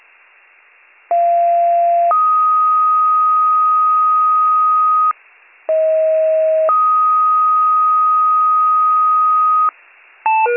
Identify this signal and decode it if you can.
Two-tone page, heard as a one-second tone, then a three-second tone, about 690 then 1250 Hz, about 640 then 1250 Hz. It is heard from 1.1 to 9.7 s.